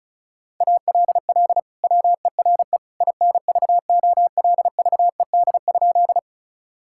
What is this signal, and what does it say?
Morse code, heard as a keyed tone, about 34 wpm, 700 Hz. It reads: ALL WERE INVOLVED?
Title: Sentences from Top 1000 Words - Review Part 5 - 35wpm